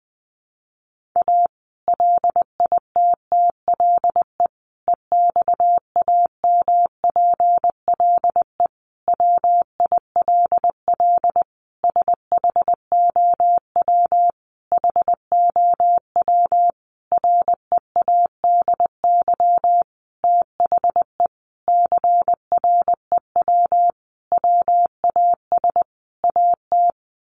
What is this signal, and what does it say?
Morse code, heard as a keyed tone, about 20 wpm, 700 Hz. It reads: A LITTLE EXAMPLE WILL SHOW HOW READY THE CREW WAS AT
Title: Sentences from Top 1500 Words 20wpm